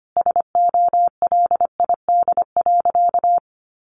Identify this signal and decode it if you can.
Morse code, heard as a keyed tone, about 25 wpm, 700 Hz. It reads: SOLID.
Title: Sentences from Top 1500 Words 25wpm